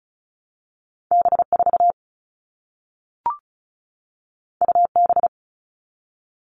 Morse code, heard as a keyed tone, about 36 wpm, 700 Hz. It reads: B4 UB
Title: All Cut Numbers - 2-Digit 35wpm